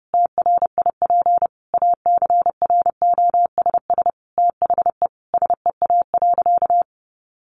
Morse code, heard as a keyed tone, about 28 wpm, 700 Hz. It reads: TRIP ACROSS THE SEA.
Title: Sentences from Top 1300 Words 30wpm